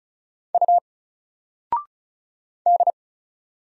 Morse code, heard as a keyed tone, about 36 wpm, 700 Hz. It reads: U D
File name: Single-Letter-Number-35wpm